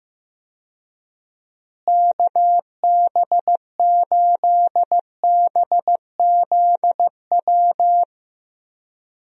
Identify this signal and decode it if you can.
Morse code, heard as a keyed tone, about 15 wpm, 700 Hz. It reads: KB8BZW
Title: USA Call Signs - 6 Characters - Repeated 3x - 15wpm